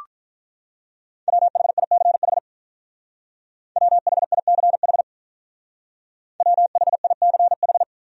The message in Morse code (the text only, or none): WHICH WHICH WHICH